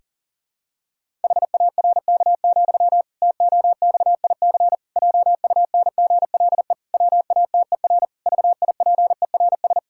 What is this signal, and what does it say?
Morse code, heard as a keyed tone, about 40 wpm, 700 Hz. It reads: HARK, TOXIC JUNGLE WATER VIPERS
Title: Pangram Challenge v2 - 40wpm